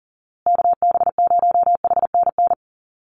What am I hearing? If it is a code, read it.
Morse code, heard as a keyed tone, about 40 wpm, 700 Hz. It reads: KB0HNN
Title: USA Call Signs Repeated 2x 40wpm